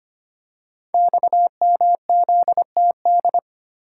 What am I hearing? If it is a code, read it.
Morse code, heard as a keyed tone, about 25 wpm, 700 Hz. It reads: XMZTD
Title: Code Groups - 5 Letter:Numbers - 25wpm